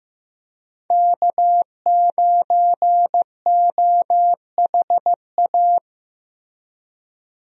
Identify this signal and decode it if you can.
Morse code, heard as a keyed tone, about 15 wpm, 700 Hz. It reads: K9OHA
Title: USA Call Signs - 5 Characters - Repeated 3x - 15wpm